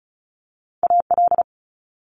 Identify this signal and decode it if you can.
Morse code, heard as a keyed tone, about 33 wpm, 700 Hz. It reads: AL
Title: Multi-letter Phonemes Quick Fire 35wpm